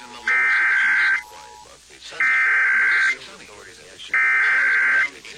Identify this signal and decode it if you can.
Emergency Alert System (SAME) tones (header bursts), heard as harsh buzzing bursts, with speech. It is heard from 0.3 to 5.0 s.